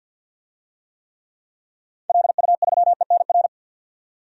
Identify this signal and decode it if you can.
Morse code, heard as a keyed tone, about 51 wpm, 700 Hz. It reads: LU3ENR
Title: DXCC-100 Call Signs - 6 Characters - 50wpm